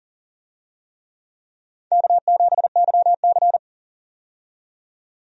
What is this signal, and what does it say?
Morse code, heard as a keyed tone, about 37 wpm, 700 Hz. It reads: K7YC
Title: USA Call Signs - 4 Characters - 40wpm